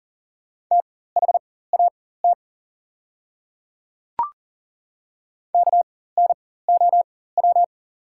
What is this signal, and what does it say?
Morse code, heard as a keyed tone, about 40 wpm, spaced as for 19 wpm, 690 Hz. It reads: THAT KNOW
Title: 4x Character Spacing - Top 100 Words - 40wpm